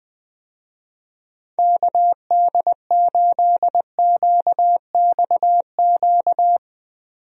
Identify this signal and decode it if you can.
Morse code, heard as a keyed tone, about 20 wpm, 700 Hz. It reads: KD8QXQ
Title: USA Call Signs - 6 Characters - 20wpm